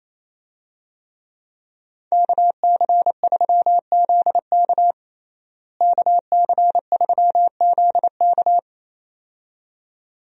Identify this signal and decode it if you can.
Morse code, heard as a keyed tone, about 28 wpm, 700 Hz. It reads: KC3ZK KC3ZK
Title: USA Call Signs - 5 Characters - Repeated 2x - 28wpm